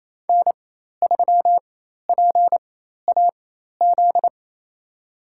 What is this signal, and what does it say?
Morse code, heard as a keyed tone, about 28 wpm, spaced as for 13 wpm, 700 Hz. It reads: N3PAZ
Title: USA Call Signs - 5 Characters - 4x Character Spacing - 28wpm